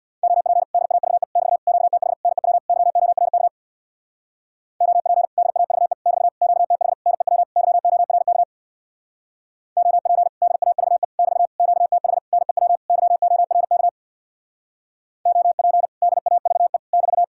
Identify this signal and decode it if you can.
Morse code, heard as a keyed tone, about 60 wpm, 700 Hz. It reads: OP DAVE <BT> QTH NEW YORK OP DAVE <BT> QTH NEW YORK OP DAVE <BT> QTH NEW YORK OP DAVE <BT>
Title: Speed-Racing Common QSO Phrases - 50wpm